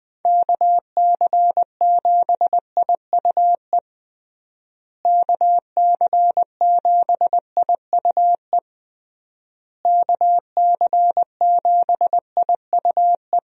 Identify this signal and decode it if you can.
Morse code, heard as a keyed tone, about 20 wpm, 700 Hz. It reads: KC7IUE KC7IUE KC7IUE
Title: USA Call Signs - 6 Characters - Repeated 3x - 20wpm